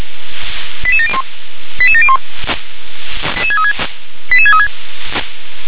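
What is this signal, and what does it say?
Five-tone selective call (ZVEI), heard as quick runs of five tones, tones of about 70 ms. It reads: 88701 88701 88636
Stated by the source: A ZVEI five-tone selective call received by SDR.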